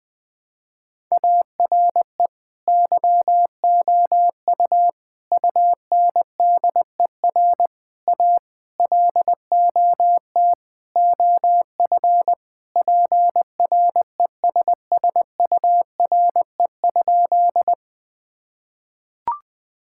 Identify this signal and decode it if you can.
Morse code, heard as a keyed tone, about 20 wpm, 700 Hz. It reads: ARE YOU UNDER A LOT OF PRESSURE?
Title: Sentences from Top 1000 Words - Review Part 6 - 20wpm